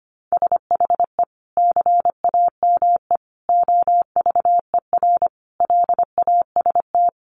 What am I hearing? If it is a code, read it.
Morse code, heard as a keyed tone, about 25 wpm, 700 Hz. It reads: SHE CAME OVER LAST
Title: Sentences from Top 300 Words 25wpm